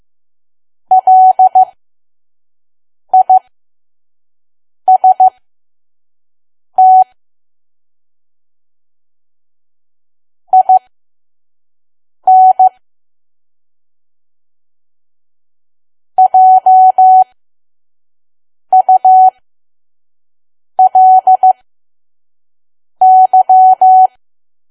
Morse code, heard as a keyed tone, about 15 wpm, spaced as for 5 wpm, 750 Hz. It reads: LIST IN JULY
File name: Archive / 5 WPM files / PM